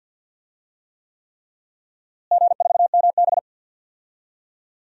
Morse code, heard as a keyed tone, about 49 wpm, 700 Hz. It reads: G4MB
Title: DXCC-100 Call Signs - 4 Characters - Repeated 3x - 50wpm